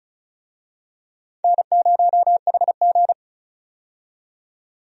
Morse code, heard as a keyed tone, about 36 wpm, 700 Hz. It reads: N0HG
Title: USA Call Signs - 4 Characters - 35wpm